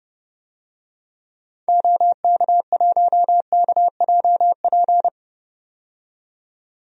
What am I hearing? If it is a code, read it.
Morse code, heard as a keyed tone, about 31 wpm, 700 Hz. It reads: OK1KJP
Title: DXCC-100 Call Signs - 6 Characters - 30wpm